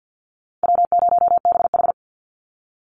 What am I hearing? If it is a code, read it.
Morse code, heard as a keyed tone, about 50 wpm, 700 Hz. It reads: W0BH
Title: Easy Call Signs Repeated 2x - 50wpm